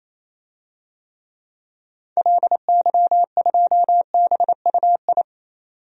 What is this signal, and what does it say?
Morse code, heard as a keyed tone, about 28 wpm, 700 Hz. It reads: LY2BUI
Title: DXCC-100 Call Signs - 6 Characters - 28wpm